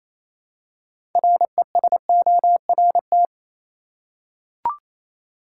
Morse code, heard as a keyed tone, about 28 wpm, 700 Hz. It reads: RESORT
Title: Six Letter Words - 28wpm